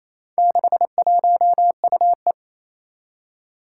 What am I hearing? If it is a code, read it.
Morse code, heard as a keyed tone, about 27 wpm, 700 Hz. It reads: 61UE
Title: Four Challenging Letter-Number Combinations 28wpm